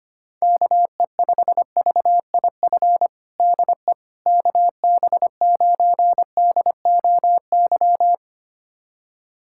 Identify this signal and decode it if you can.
Morse code, heard as a keyed tone, about 25 wpm, 700 Hz. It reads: KE5VIF DE KB9DOY
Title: USA Callsign DE Callsign - 6 Characters - 25wpm